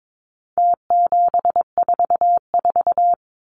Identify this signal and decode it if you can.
Morse code, heard as a keyed tone, about 22 wpm, 700 Hz. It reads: T744
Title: Numbers - Contest Style - 4-Digit 22wpm